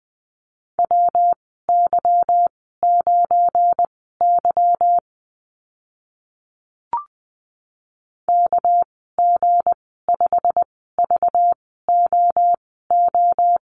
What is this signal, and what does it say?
Morse code, heard as a keyed tone, about 20 wpm, spaced as for 15 wpm, 700 Hz. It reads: WY9Y KG5VOO
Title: 207 Lesson 33g - Letter j - Call Signs 20wpm